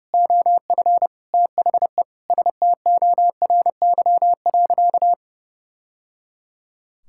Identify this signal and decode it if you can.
Morse code, heard as a keyed tone, about 30 wpm, 700 Hz. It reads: OF THE STORY.
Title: Sentences from Top 1000 Words - Review Part 6 - 30wpm